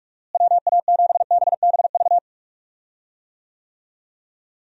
Morse code, heard as a keyed tone, about 44 wpm, 700 Hz. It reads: WA7BBV
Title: USA Call Signs - 6 Characters - Repeated 3x - 45wpm